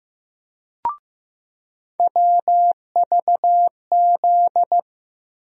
Morse code, heard as a keyed tone, about 15 wpm, 700 Hz. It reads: WVZ
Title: USA Call Sign Postfixes - 3 Letters 15wpm